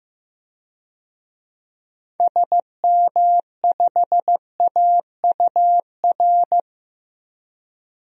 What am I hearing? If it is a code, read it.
Morse code, heard as a keyed tone, about 15 wpm, 700 Hz. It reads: SM5AUR
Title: DXCC-100 Call Signs - 6 Characters - 15wpm